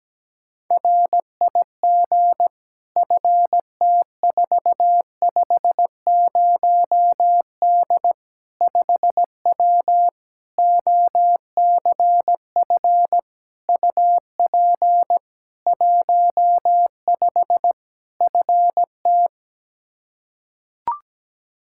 Morse code, heard as a keyed tone, about 17 wpm, 700 Hz. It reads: RIG FT450D 5W OCF UP 15 FT
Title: K1USN SST QSO Practice 17wpm